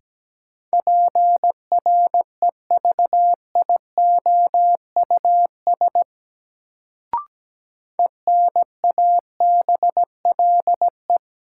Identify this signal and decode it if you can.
Morse code, heard as a keyed tone, about 17 wpm, 700 Hz. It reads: PREVIOUS ENABLE
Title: Sentences from Top 1500 Words - New Word Drill Alt - 17wpm